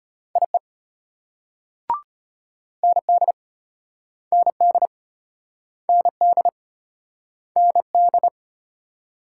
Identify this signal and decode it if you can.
Morse code, heard as a keyed tone, about 35 wpm, 700 Hz. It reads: IE ND ND ND ND